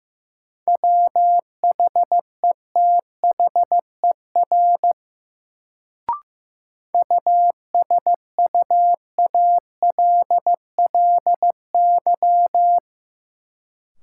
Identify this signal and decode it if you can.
Morse code, heard as a keyed tone, about 15 wpm, 700 Hz. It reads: WHETHER USUALLY
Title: Sentences from Top 1000 Words - Review Part 3 - New Word Drill - 15wpm